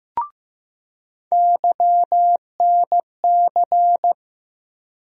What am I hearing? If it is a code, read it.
Morse code, heard as a keyed tone, about 15 wpm, 700 Hz. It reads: YNC
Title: USA Call Sign Postfixes - 3 Letters 15wpm